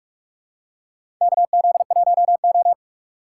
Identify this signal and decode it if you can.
Morse code, heard as a keyed tone, about 48 wpm, 700 Hz. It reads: KZ1O
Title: USA Call Signs - 4 Characters - 3x Character Spacing - 45wpm